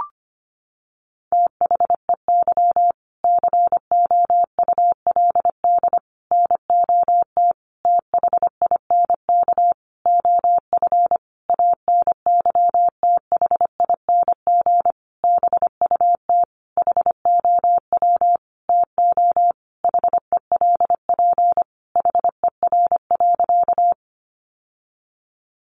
Morse code, heard as a keyed tone, about 25 wpm, 700 Hz. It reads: THEY COULD NOT THINK OF ANYTHING BUT HOW TO HELP HER.
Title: Sentences from Top 1000 Words 25wpm